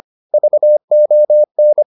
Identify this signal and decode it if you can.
Morse code, heard as a keyed tone, about 25 wpm, 600 Hz. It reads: VON